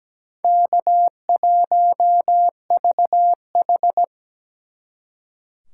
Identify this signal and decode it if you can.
Morse code, heard as a keyed tone, about 20 wpm, 700 Hz. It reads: K1VH
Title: Speed Racing - USA 1x2 Call Signs 17wpm